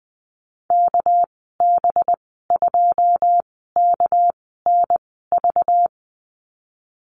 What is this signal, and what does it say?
Morse code, heard as a keyed tone, about 20 wpm, spaced as for 15 wpm, 700 Hz. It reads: KB2KNV
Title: 207 Lesson 33g - Letter j - Call Signs 20wpm